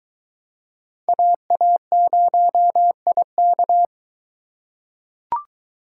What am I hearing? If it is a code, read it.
Morse code, heard as a keyed tone, about 23 wpm, 700 Hz. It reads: AA0IK